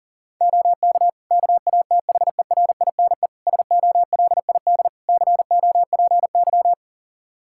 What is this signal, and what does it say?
Morse code, heard as a keyed tone, about 40 wpm, 700 Hz. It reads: OK KATHERINE SOLID COPY
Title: Practice QSO - Exchange 3 - More Info - 40wpm